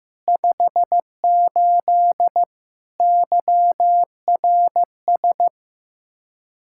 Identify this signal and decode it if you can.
Morse code, heard as a keyed tone, about 15 wpm, 700 Hz. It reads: 58 YRS
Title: Practice QSO - Exchange 3 - More Info - 15wpm